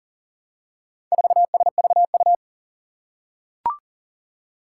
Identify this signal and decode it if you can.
Morse code, heard as a keyed tone, about 40 wpm, 700 Hz. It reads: 4SVU
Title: Four Challenging Letter-Number Combinations 40wpm